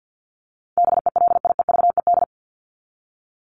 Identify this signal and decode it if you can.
Morse code, heard as a keyed tone, about 50 wpm, 700 Hz. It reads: BELIEVED
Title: ED Suffix - Top 100 Words with Base Words 50wpm